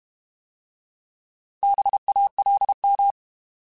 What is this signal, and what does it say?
Morse code, heard as a keyed tone, about 32 wpm, 800 Hz. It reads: BALM
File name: Alphabetical / Alphabetical mp3s / Bs @30